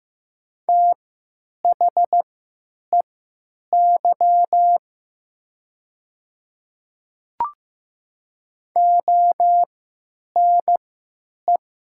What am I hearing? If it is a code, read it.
Morse code, heard as a keyed tone, about 15 wpm, spaced as for 9 wpm, 700 Hz. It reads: THEY ONE
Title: 3x Character Spacing - Top 100 Words - 15wpm